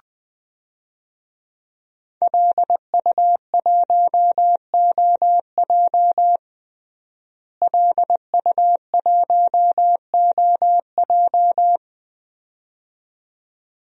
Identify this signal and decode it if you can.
Morse code, heard as a keyed tone, about 20 wpm, 700 Hz. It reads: LU1OJ LU1OJ
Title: DXCC-100 Call Signs - 5 Characters - Repeated 2x - 20wpm